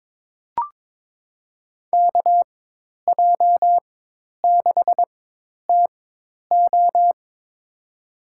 Morse code, heard as a keyed tone, about 22 wpm, spaced as for 10 wpm, 700 Hz. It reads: KJ6TO
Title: USA Call Signs - 5 Characters - 4x Character Spacing - 22wpm